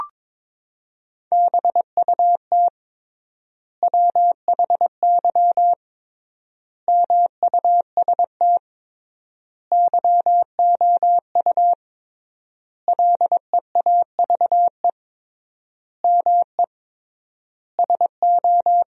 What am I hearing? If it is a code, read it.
Morse code, heard as a keyed tone, about 22 wpm, 700 Hz. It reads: BUT WHY MUST YOU LEAVE ME SO
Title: 3x Word-Spacing - Sentences from Top 400 Words - 22wpm